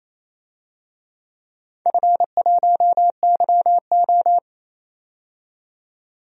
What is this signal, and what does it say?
Morse code, heard as a keyed tone, about 28 wpm, 700 Hz. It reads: F1YO
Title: DXCC-100 Call Signs - 4 Characters - 28wpm